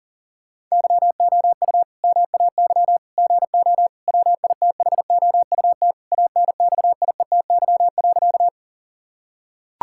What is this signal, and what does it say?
Morse code, heard as a keyed tone, about 40 wpm, 700 Hz. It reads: YOU MAY GO WITHOUT ANXIETY.
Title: Sentences from Top 1600 Words 40wpm